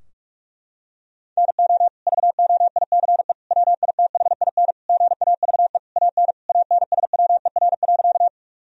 Morse code, heard as a keyed tone, about 41 wpm, 700 Hz. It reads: NO VOICE WITHIN GAVE AN ANSWER.
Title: Sentences from Top 1000 Words - Review Part 4 - 45wpm